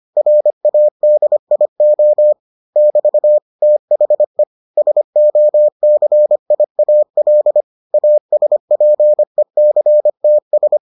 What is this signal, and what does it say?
Morse code, heard as a keyed tone, about 25 wpm, 600 Hz. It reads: RADIO <BT> THE SOCIAL ASPECTS